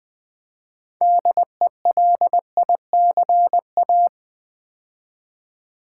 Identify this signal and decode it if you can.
Morse code, heard as a keyed tone, about 20 wpm, 700 Hz. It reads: DELICA
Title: Build A Word - 3 Letters At A Time - 8-Letter Words - 20wpm